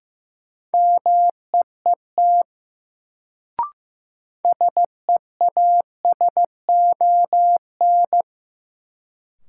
Morse code, heard as a keyed tone, about 16 wpm, 700 Hz. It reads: MEET SEASON
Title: Sets of 5 Words - New Word Drill - 15wpm